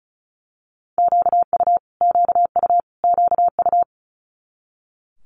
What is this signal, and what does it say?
Morse code, heard as a keyed tone, about 36 wpm, 700 Hz. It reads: QU QU QU
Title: Multi-letter Phonemes Repeated 3x 35wpm